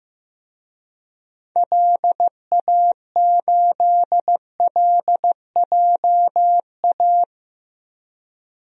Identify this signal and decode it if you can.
Morse code, heard as a keyed tone, about 15 wpm, 700 Hz. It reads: LA8LJA
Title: DXCC-100 Call Signs - 6 Characters - Repeated 2x - 15wpm